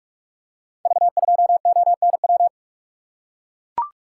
Morse code, heard as a keyed tone, about 44 wpm, 700 Hz. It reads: V2ONW
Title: Code Groups - 5 Letter:Numbers - 45wpm